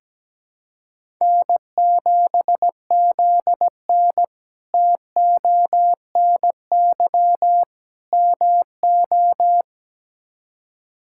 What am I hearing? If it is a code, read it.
Morse code, heard as a keyed tone, about 17 wpm, 700 Hz. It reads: N7ZN TONY MO
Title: K1USN SST QSO Practice - 17wpm